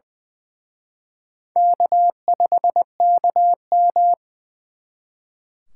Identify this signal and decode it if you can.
Morse code, heard as a keyed tone, about 25 wpm, 700 Hz. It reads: K5KM
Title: Speed Racing - USA 1x2 Call Signs 20wpm